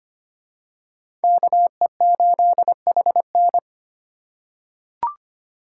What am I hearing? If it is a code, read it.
Morse code, heard as a keyed tone, about 25 wpm, 700 Hz. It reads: KE8HN
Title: USA Call Signs - 4x Character Spacing - 25wpm